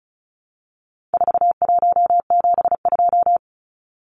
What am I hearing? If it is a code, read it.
Morse code, heard as a keyed tone, about 34 wpm, 700 Hz. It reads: 4172
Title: Numbers - 4-Digit 35wpm